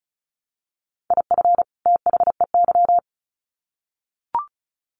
Morse code, heard as a keyed tone, about 34 wpm, 700 Hz. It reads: IF THEY
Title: Sets of 2 Words 35wpm